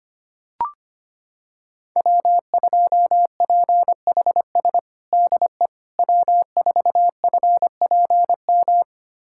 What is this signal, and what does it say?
Morse code, heard as a keyed tone, about 25 wpm, 700 Hz. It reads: W2PHS DE W4FPM
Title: USA Callsign DE Callsign - 5 Characters - 25wpm